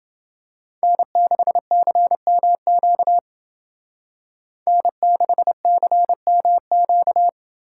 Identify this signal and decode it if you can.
Morse code, heard as a keyed tone, about 29 wpm, 700 Hz. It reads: N6CMQ N6CMQ